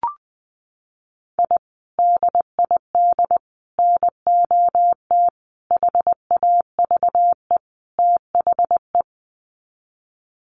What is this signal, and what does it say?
Morse code, heard as a keyed tone, about 20 wpm, 700 Hz. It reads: I DID NOT HAVE THE
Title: Sets of 5 Words Encore 20wpm